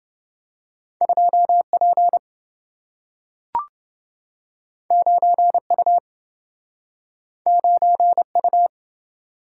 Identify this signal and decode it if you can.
Morse code, heard as a keyed tone, about 27 wpm, 700 Hz. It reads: 2 P 9 U 9 U